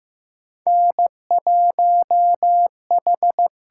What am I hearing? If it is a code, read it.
Morse code, heard as a keyed tone, about 15 wpm, 700 Hz. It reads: N1H
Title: USA Call Signs - 4 Characters - 6x Character Spacing - 15wpm